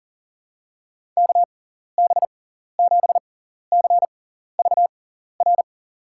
Morse code, heard as a keyed tone, about 40 wpm, spaced as for 14 wpm, 700 Hz. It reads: KB7CVR
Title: USA Call Signs - 6 Characters - 6x Character Spacing - 40wpm